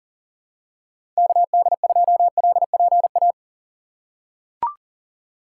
Morse code, heard as a keyed tone, about 40 wpm, 700 Hz. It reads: KD2LPA
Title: USA Call Signs - 6 Characters - 2x Character Spacing - 40wpm